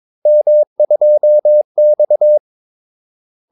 Morse code, heard as a keyed tone, about 22 wpm, 600 Hz. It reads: M2X
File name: Callsigns-22wpm